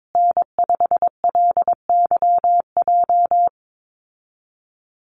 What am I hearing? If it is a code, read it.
Morse code, heard as a keyed tone, about 22 wpm, 700 Hz. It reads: N5LYJ
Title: Call Signs Repeated 3x - 22wpm